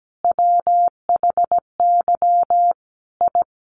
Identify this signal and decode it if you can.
Morse code, heard as a keyed tone, about 17 wpm, 700 Hz. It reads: WHY I
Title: Sets of 5 Words Encore 17wpm